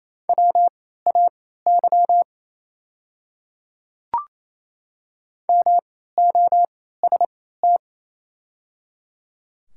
Morse code, heard as a keyed tone, about 27 wpm, spaced as for 16 wpm, 700 Hz. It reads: WAY MOST
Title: 3x Character Spacing - Top 100 Words - 28wpm